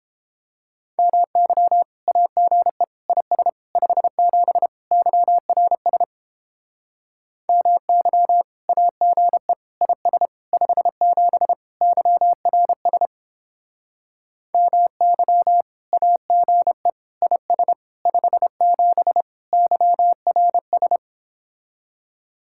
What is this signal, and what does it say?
Morse code, heard as a keyed tone, about 29 wpm, 700 Hz. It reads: MY AGE IS 57 YRS MY AGE IS 57 YRS MY AGE IS 57 YRS